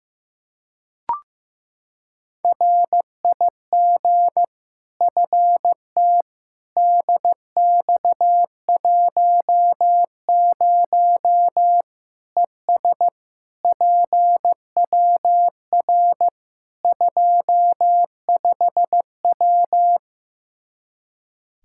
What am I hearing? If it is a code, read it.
Morse code, heard as a keyed tone, about 16 wpm, 700 Hz. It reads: RIG FT DX10 ES PWR 25W
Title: Practice QSO - Exchange 2 - Info - 15wpm